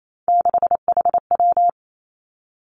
Morse code, heard as a keyed tone, about 28 wpm, 700 Hz. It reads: 6HW
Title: Three Challenging Letter-Number Combinations 28wpm